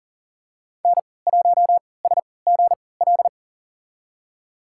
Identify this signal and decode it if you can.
Morse code, heard as a keyed tone, about 40 wpm, spaced as for 23 wpm, 700 Hz. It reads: N1SGL
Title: USA Call Signs - 5 Characters - 3x Character Spacing - 40wpm